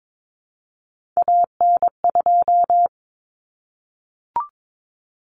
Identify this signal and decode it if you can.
Morse code, heard as a keyed tone, about 22 wpm, 700 Hz. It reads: AN2
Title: Common Cut Numbers - 3-Digit 22wpm